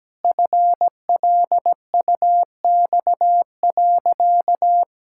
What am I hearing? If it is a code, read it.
Morse code, heard as a keyed tone, about 17 wpm, 700 Hz. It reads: FLUX.
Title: Pangram Challenge v2 - 17wpm